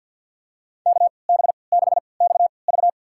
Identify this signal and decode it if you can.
Morse code, heard as a keyed tone, about 53 wpm, spaced as for 29 wpm, 700 Hz. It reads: KB6XV
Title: USA Call Signs - 5 Characters - 3x Character Spacing - 50wpm